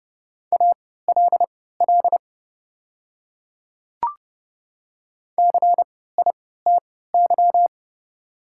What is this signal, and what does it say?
Morse code, heard as a keyed tone, about 30 wpm, spaced as for 17 wpm, 690 Hz. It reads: ALL CITY
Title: 3x Character Spacing - Top 300 Words - 30wpm